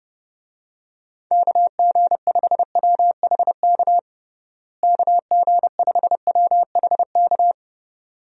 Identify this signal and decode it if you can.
Morse code, heard as a keyed tone, about 30 wpm, 700 Hz. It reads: KG5WHK KG5WHK
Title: USA Call Signs - 6 Characters - Repeated 2x - 30wpm